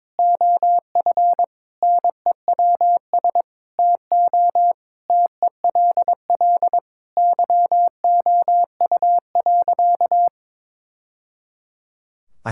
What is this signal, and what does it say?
Morse code, heard as a keyed tone, about 22 wpm, 700 Hz. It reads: OF NEWS TO TELL YOU.
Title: Sentences from Top 1000 Words - Review Part 5 - 22wpm